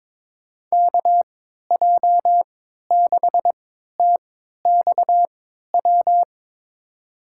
Morse code, heard as a keyed tone, about 22 wpm, spaced as for 12 wpm, 700 Hz. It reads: KJ6TXW
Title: USA Call Signs - 3x Character Spacing - 22wpm